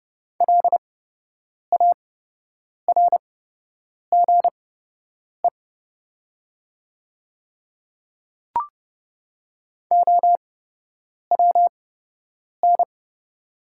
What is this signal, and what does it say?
Morse code, heard as a keyed tone, about 30 wpm, spaced as for 8 wpm, 700 Hz. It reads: LARGE OWN
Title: 8x Character Spacing - Top 300 Words - 30wpm